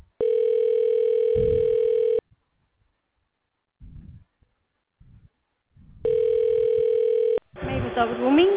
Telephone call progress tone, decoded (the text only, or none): ringback tone